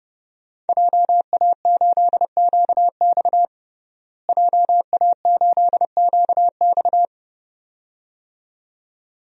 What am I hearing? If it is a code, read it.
Morse code, heard as a keyed tone, about 31 wpm, 700 Hz. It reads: JA8QX JA8QX
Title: DXCC-100 Call Signs - 5 Characters - Repeated 2x - 30wpm